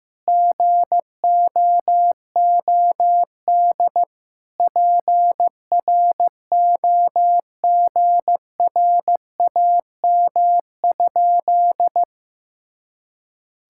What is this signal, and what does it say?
Morse code, heard as a keyed tone, about 15 wpm, 700 Hz. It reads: GOOD PROGRAM?
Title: Sentences from Top 1000 Words - Review Part 5 - 15wpm